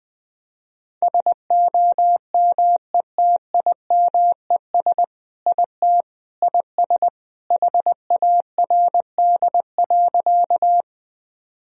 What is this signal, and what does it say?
Morse code, heard as a keyed tone, about 21 wpm, 700 Hz. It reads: SOMETIMES IT IS HARD.
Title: Sentences from Top 1000 Words - Review Part 3 - 20wpm